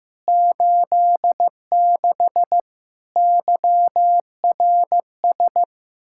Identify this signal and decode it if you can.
Morse code, heard as a keyed tone, about 15 wpm, 700 Hz. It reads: 86 YRS
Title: Practice QSO - Exchange 3 - More Info - 15wpm